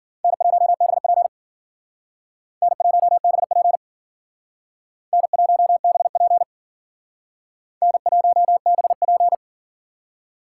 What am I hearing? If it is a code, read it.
Morse code, heard as a keyed tone, about 53 wpm, 700 Hz. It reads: N1BP N1BP N1BP N1BP